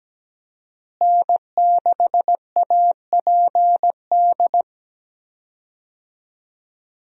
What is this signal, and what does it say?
Morse code, heard as a keyed tone, about 17 wpm, 700 Hz. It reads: N6APD
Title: USA Call Signs - 5 Characters - Repeated 2x - 17wpm